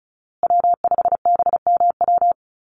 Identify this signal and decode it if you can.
Morse code, heard as a keyed tone, about 34 wpm, 700 Hz. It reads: W5BMW
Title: USA Call Signs Repeated 2x 35wpm